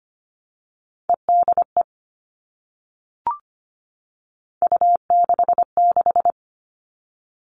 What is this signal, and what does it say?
Morse code, heard as a keyed tone, about 25 wpm, 700 Hz. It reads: EDE U66
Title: All Cut Numbers - 3-Digit 25wpm